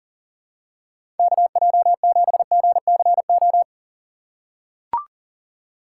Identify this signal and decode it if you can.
Morse code, heard as a keyed tone, about 40 wpm, 700 Hz. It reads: KJ7GCO
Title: USA Call Signs - 2x Character Spacing - 40wpm